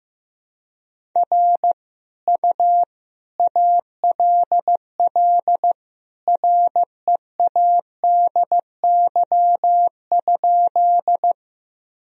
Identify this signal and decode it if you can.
Morse code, heard as a keyed tone, about 15 wpm, 700 Hz. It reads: R U ALL READY?
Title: Sentences from Top 400 Words and Abbreviations 15wpm